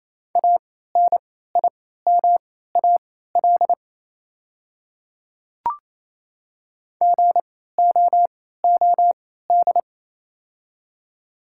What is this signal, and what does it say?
Morse code, heard as a keyed tone, about 28 wpm, spaced as for 16 wpm, 700 Hz. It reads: ANIMAL GOOD
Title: 3x Character Spacing - Top 300 Words - 28wpm